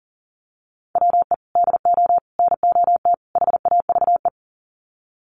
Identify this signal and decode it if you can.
Morse code, heard as a keyed tone, about 40 wpm, 700 Hz. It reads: WE DO NOT HAVE
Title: Sets of 4 Words 40wpm